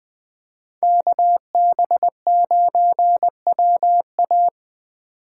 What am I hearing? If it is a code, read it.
Morse code, heard as a keyed tone, about 23 wpm, 700 Hz. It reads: KB9WA
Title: Speed Racing - USA 2x2 Call Signs 20wpm